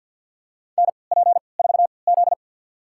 Morse code, heard as a keyed tone, about 52 wpm, spaced as for 28 wpm, 700 Hz. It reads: NP4Z
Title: USA Call Signs - 4 Characters - 3x Character Spacing - 50wpm